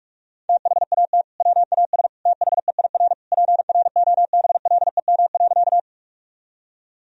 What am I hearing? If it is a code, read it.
Morse code, heard as a keyed tone, about 46 wpm, 700 Hz. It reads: THAT WAS THEIR PROBLEM.
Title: Sentences from Top 1000 Words - Review Part 2 - 45wpm